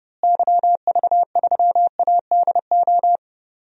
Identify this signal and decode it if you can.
Morse code, heard as a keyed tone, about 32 wpm, 700 Hz. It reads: YV3ADO
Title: DXCC-100 Call Signs - 6 Characters - Repeated 3x - 30wpm